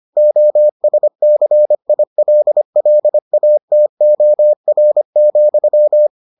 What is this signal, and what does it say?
Morse code, heard as a keyed tone, about 25 wpm, 600 Hz. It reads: OSCILLATOR,